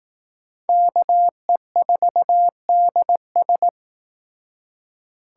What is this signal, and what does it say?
Morse code, heard as a keyed tone, about 18 wpm, 700 Hz. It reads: KE4DS
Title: Speed Racing - USA 2x2 Call Signs 15wpm